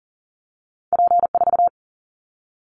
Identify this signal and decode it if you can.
Morse code, heard as a keyed tone, about 40 wpm, 700 Hz. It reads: P4
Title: Callsign Prefixes for Top 100 DXCC - 40wpm